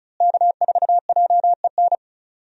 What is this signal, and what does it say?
Morse code, heard as a keyed tone, about 36 wpm, 700 Hz. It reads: K4JEN
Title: USA Call Signs - 5 Characters - Repeated 3x - 35wpm